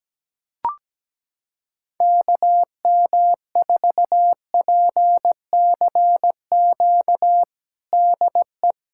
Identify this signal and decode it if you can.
Morse code, heard as a keyed tone, about 17 wpm, 700 Hz. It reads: KM4PCQ DE
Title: USA Callsign DE Callsign - 6 Characters - 17wpm